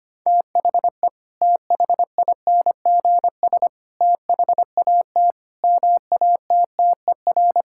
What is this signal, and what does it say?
Morse code, heard as a keyed tone, about 25 wpm, 700 Hz. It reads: THE THINGS THAT MATTER
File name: practice / 25wpm / Sentences with 3-Word Phrases - 25wpm